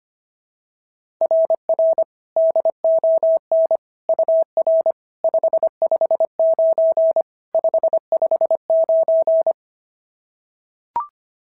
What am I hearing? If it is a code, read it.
Morse code, heard as a keyed tone, about 25 wpm, 690 Hz. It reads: RR DON UR 559 559
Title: SOTA QSO Practice 25wpm